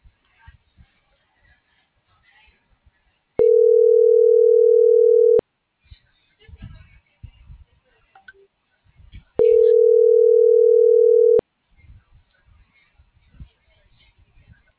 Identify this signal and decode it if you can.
Ringback tone (North America), 440 + 480 Hz.
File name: CENTRAL FRALSE / out gerardo2024sub